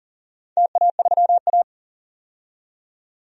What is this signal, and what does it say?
Morse code, heard as a keyed tone, about 42 wpm, 700 Hz. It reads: TA3A
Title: DXCC-100 Call Signs - 4 Characters - 40wpm